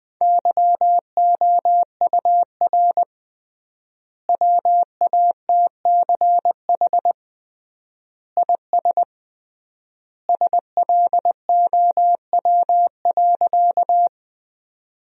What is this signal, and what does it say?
Morse code, heard as a keyed tone, about 21 wpm, 700 Hz. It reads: YOUR WATCH IS SLOW.
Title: 3x Word-Spacing - Sentences from Top 400 Words - 20wpm